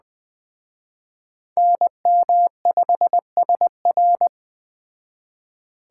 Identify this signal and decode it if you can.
Morse code, heard as a keyed tone, about 20 wpm, 700 Hz. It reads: NM5SR